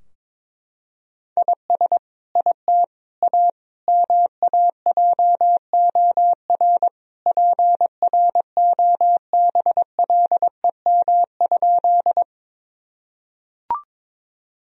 Morse code, heard as a keyed tone, about 22 wpm, 700 Hz. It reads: IS IT A MAJOR PROBLEM?
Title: Sentences from Top 1000 Words - Review Part 4 - 22wpm